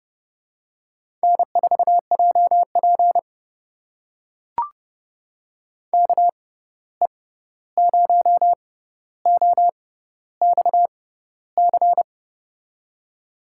Morse code, heard as a keyed tone, about 32 wpm, spaced as for 14 wpm, 700 Hz. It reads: N4JP KE0OXC
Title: USA Call Signs - 6x Character Spacing - 30wpm